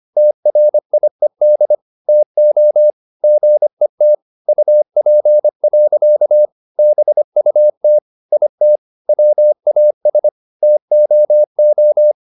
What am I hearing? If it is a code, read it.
Morse code, heard as a keyed tone, about 25 wpm, 600 Hz. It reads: TRIED TO GET UP. BUT IT WAS TOO